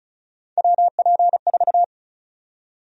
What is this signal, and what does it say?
Morse code, heard as a keyed tone, about 36 wpm, 700 Hz. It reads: WP4
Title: USA Call Sign Prefixes - Repeated 2x - 35wpm